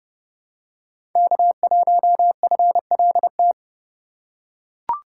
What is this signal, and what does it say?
Morse code, heard as a keyed tone, about 31 wpm, 700 Hz. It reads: K1FLT
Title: USA Call Signs - 5 Characters - 4x Character Spacing - 30wpm